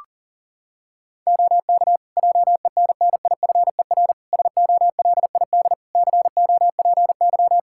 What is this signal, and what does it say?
Morse code, heard as a keyed tone, about 40 wpm, 700 Hz. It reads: OK JENNIFER SOLID COPY
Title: Practice QSO - Exchange 3 - More Info - 40wpm